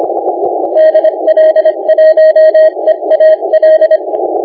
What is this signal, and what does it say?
Morse code, heard as a keyed tone, about 23 wpm, 630 Hz. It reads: DL1EAL
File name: la4c / sac-cw-2018 / CONTEST AUDIO 2200Z